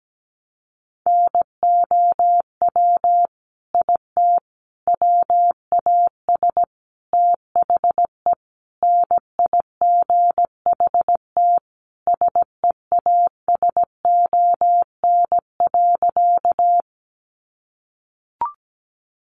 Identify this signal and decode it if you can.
Morse code, heard as a keyed tone, about 17 wpm, 700 Hz. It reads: NOW IT WAS THE NIGHT SEASON.
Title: Sentences from Top 1400 Words 17wpm